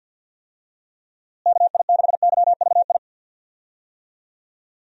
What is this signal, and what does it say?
Morse code, heard as a keyed tone, about 48 wpm, 700 Hz. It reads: KI6YVI
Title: USA Call Signs - 6 Characters - Repeated 2x - 50wpm